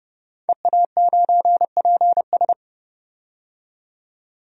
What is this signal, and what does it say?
Morse code, heard as a keyed tone, about 32 wpm, 700 Hz. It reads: EA9PS
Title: DXCC-100 Call Signs - 5 Characters - 30wpm